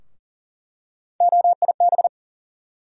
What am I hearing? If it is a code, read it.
Morse code, heard as a keyed tone, about 40 wpm, 700 Hz. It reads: OIB
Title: USA Call Sign Postfixes - 3 Letters 40wpm